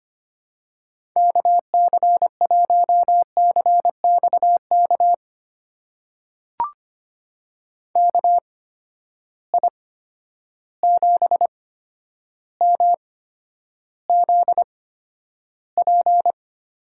Morse code, heard as a keyed tone, about 25 wpm, spaced as for 11 wpm, 700 Hz. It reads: KC1CXK KI7MZP
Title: USA Call Signs - 6 Characters - 8x Character Spacing - 25wpm